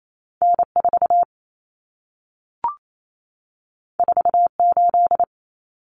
Morse code, heard as a keyed tone, about 28 wpm, 700 Hz. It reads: N4 48
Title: Common Cut Numbers - 2-Digit 28wpm